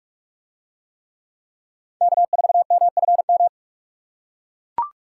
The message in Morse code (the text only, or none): K4MFM